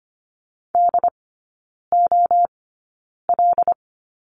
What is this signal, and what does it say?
Morse code, heard as a keyed tone, about 25 wpm, spaced as for 9 wpm, 690 Hz. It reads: DOL
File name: Lesson 15e - Number 5 - Three Characters 25wpm